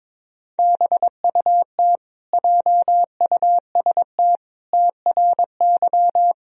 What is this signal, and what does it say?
Morse code, heard as a keyed tone, about 22 wpm, 700 Hz. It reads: BUT JUST TRY
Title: Sentences from Top 1000 Words - Review Part 1 - 22wpm